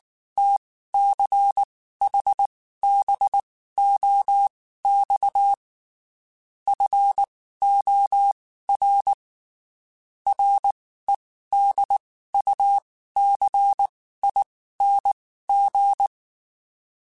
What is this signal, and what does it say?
Morse code, heard as a keyed tone, about 19 wpm, spaced as for 14 wpm, 800 Hz. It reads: TCHBOX FOR REDUCING
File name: QSOs / QSOs-mp3s / WD0FBT 1@18